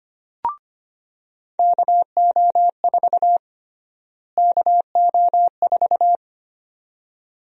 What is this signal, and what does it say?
Morse code, heard as a keyed tone, about 25 wpm, 700 Hz. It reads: KO4 KO4
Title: USA Call Sign Prefixes - Repeated 2x - 25wpm